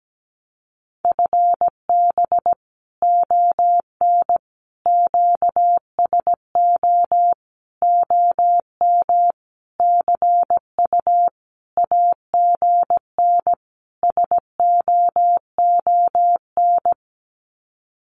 Morse code, heard as a keyed tone, about 17 wpm, 700 Hz. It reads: FB ON QSO OM CU AGN SOON
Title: Common QSO Phrases 17wpm